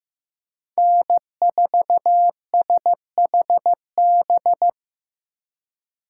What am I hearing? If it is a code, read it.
Morse code, heard as a keyed tone, about 15 wpm, 700 Hz. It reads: N4SHB
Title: USA Call Signs - 5 Characters - 4x Character Spacing - 15wpm